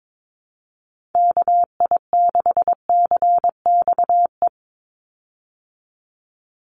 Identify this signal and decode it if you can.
Morse code, heard as a keyed tone, about 22 wpm, 700 Hz. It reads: KI6CXE
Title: USA Call Signs 22wpm